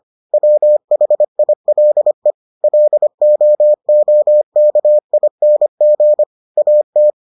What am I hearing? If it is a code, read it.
Morse code, heard as a keyed tone, about 25 wpm, 600 Hz. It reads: WHILE LOOKING AT